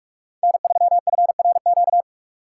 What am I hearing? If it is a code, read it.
Morse code, heard as a keyed tone, about 44 wpm, 700 Hz. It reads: N3FRQ
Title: USA Call Signs - 5 Characters - 8x Character Spacing - 45wpm